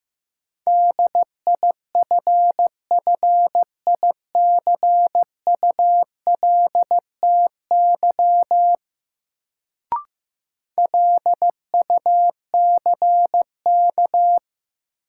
Morse code, heard as a keyed tone, about 15 wpm, 700 Hz. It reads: DIFFICULTY LUCK
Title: Sentences from Top 1500 Words - New Word Drill - 15wpm